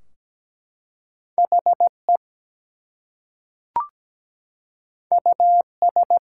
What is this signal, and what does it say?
Morse code, heard as a keyed tone, about 17 wpm, 700 Hz. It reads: HE US
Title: Two Letter Words - 17wpm